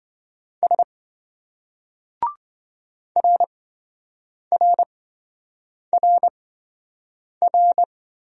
Morse code, heard as a keyed tone, about 27 wpm, spaced as for 7 wpm, 700 Hz. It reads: S RRRR